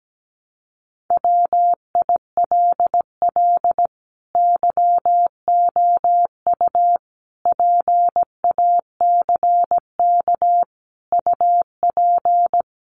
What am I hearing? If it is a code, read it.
Morse code, heard as a keyed tone, about 17 wpm, 700 Hz. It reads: WILL YOU PACK UP
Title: Sentences from Top 1300 Words 17wpm